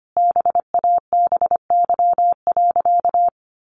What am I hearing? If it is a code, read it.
Morse code, heard as a keyed tone, about 25 wpm, 700 Hz. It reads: BABY.
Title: Sentences from Top 1400 Words 25wpm